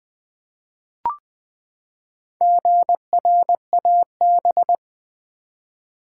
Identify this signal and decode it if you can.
Morse code, heard as a keyed tone, about 20 wpm, 700 Hz. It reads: GRAB
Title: Four Letter Words - 20wpm